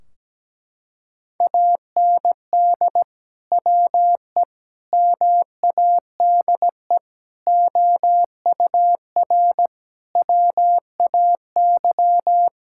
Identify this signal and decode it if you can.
Morse code, heard as a keyed tone, about 17 wpm, 700 Hz. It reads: AND WE MADE OUR WAY
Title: Sentences from Top 1600 Words 17wpm